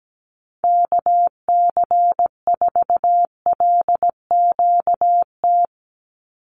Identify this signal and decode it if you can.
Morse code, heard as a keyed tone, about 17 wpm, 700 Hz. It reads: KC4LQT
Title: USA Call Signs 17wpm